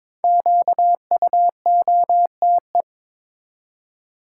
Morse code, heard as a keyed tone, about 22 wpm, 700 Hz. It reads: QUOTE
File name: 5 Letter Words - First 3 Letters Given - 22wpm